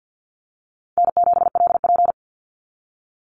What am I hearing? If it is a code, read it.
Morse code, heard as a keyed tone, about 49 wpm, 700 Hz. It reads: N7LP
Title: NA SOTA Chaser Callsigns 50wpm